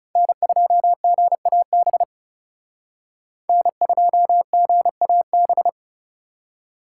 Morse code, heard as a keyed tone, about 38 wpm, 700 Hz. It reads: N2GAB N2GAB